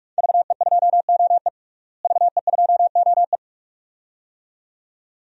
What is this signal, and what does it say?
Morse code, heard as a keyed tone, about 41 wpm, 700 Hz. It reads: VE2OE VE2OE
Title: DXCC-100 Call Signs - 5 Characters - Repeated 2x - 45wpm